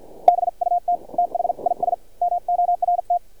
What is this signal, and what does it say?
Morse code, heard as a keyed tone, about 51 wpm, spaced as for 43 wpm, 700 Hz. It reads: BUT THIS MORT
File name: archived / CRSnet